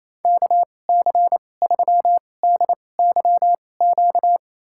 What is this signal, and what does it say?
Morse code, heard as a keyed tone, about 28 wpm, spaced as for 20 wpm, 690 Hz. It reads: KC3DYQ
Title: USA Call Signs - 6 Characters - 2x Character Spacing - 28wpm